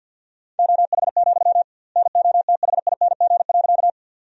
Morse code, heard as a keyed tone, about 49 wpm, 700 Hz. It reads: OH, NOTHING.
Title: 2x Word-Spacing - Sentences from Top 500 Words - 50wpm